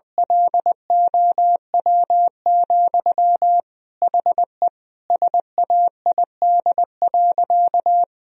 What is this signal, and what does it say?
Morse code, heard as a keyed tone, about 20 wpm, 700 Hz. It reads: LOW, HE SAID.
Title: Sentences from Top 1000 Words - Review Part 1 - 20wpm